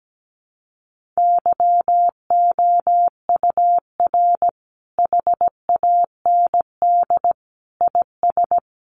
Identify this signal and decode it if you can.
Morse code, heard as a keyed tone, about 17 wpm, 700 Hz. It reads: YOUR HAND IS
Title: Sentences from Top 1500 Words 17wpm